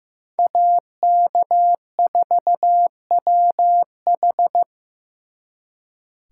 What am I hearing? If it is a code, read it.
Morse code, heard as a keyed tone, about 18 wpm, 700 Hz. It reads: AK4WH
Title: Speed Racing - USA 2x2 Call Signs 15wpm